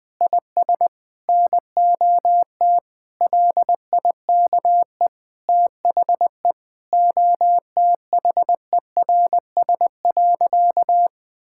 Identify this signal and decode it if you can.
Morse code, heard as a keyed tone, about 21 wpm, 700 Hz. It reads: IS NOT LIKE THE OTHERS.
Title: Sentences with 8-Word Phrases - Spoken Beginning - 20wpm